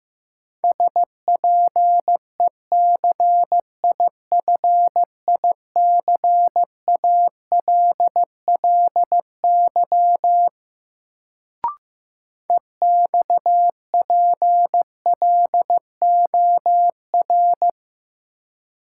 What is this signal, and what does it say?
Morse code, heard as a keyed tone, about 15 wpm, 700 Hz. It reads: SPECIFICALLY EXPLOR
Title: Sentences from Top 1100 Words - New Word Drill - 15wpm